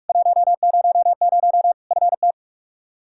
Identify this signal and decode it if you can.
Morse code, heard as a keyed tone, about 45 wpm, 700 Hz. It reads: 100 FT
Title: K1USN SST QSO Practice 45wpm